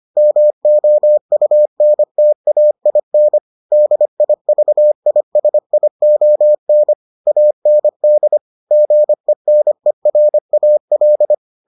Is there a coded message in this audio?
Yes: Morse code from 0.2 to 11.3 s, heard as a keyed tone, about 25 wpm, 600 Hz. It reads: MOUNTAIN DIVISION AND GENERAL